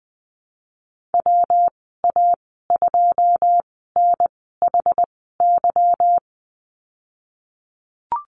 Morse code, heard as a keyed tone, about 20 wpm, spaced as for 15 wpm, 700 Hz. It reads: WA2NHY
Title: 200 Lesson 32g - Letter k - Call Signs 20wpm